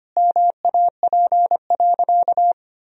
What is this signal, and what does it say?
Morse code, heard as a keyed tone, about 25 wpm, 700 Hz. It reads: MAP.
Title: Sentences from Top 1000 Words - Review Part 2 - 25wpm